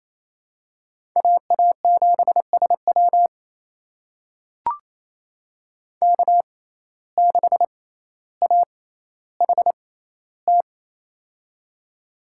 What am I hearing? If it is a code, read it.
Morse code, heard as a keyed tone, about 28 wpm, spaced as for 14 wpm, 700 Hz. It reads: AA7SW K6AHT
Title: USA Call Signs - 5 Characters - 6x Character Spacing - 28wpm